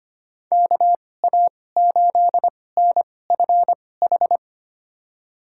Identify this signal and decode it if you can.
Morse code, heard as a keyed tone, about 25 wpm, spaced as for 18 wpm, 700 Hz. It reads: KA8NFH
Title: USA Call Signs - 6 Characters - 2x Character Spacing - 25wpm